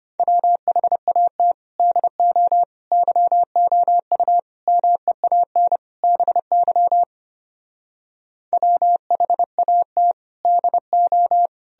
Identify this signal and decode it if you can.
Morse code, heard as a keyed tone, about 32 wpm, 700 Hz. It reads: WHAT DO YOU MEAN BY WHAT DO